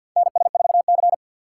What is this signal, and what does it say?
Morse code, heard as a keyed tone, about 53 wpm, 700 Hz. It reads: NS4C
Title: USA Call Signs - 3x Character Spacing - 50wpm